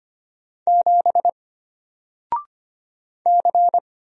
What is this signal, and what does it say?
Morse code, heard as a keyed tone, about 25 wpm, 700 Hz. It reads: 7 C
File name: Single-Letter-Number-25wpm